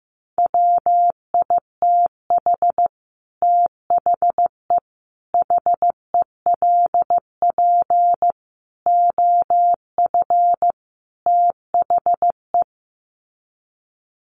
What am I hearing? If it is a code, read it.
Morse code, heard as a keyed tone, about 15 wpm, 700 Hz. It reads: WITH THE HELP OF THE
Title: Sets of 5 Words Encore 15wpm